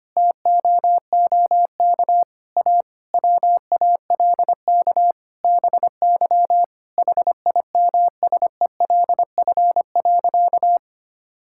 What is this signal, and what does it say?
Morse code, heard as a keyed tone, about 25 wpm, 700 Hz. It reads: TOOK A WALK BY HIMSELF.
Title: Sentences from Top 1600 Words 25wpm